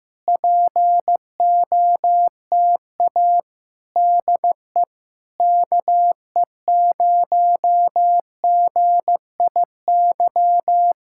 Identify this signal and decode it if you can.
Morse code, heard as a keyed tone, about 15 wpm, 700 Hz. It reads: POTA DE KE0GIY
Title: Speed-Racing Common QSO Phrases Encore - 15wpm